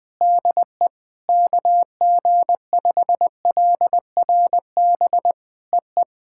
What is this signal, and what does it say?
Morse code, heard as a keyed tone, about 20 wpm, 700 Hz. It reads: DE KG5LRB EE
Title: POTA QSO Practice - 20wpm